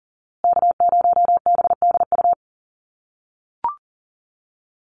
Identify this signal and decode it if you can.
Morse code, heard as a keyed tone, about 40 wpm, 700 Hz. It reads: K0BDU
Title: USA Call Signs Repeated 3x 40wpm